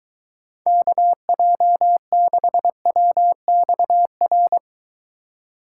Morse code, heard as a keyed tone, about 21 wpm, 700 Hz. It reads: KJ6WXR